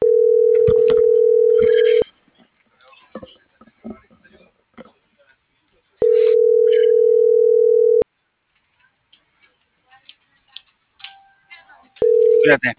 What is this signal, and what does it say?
Ringback tone (North America), 440 + 480 Hz.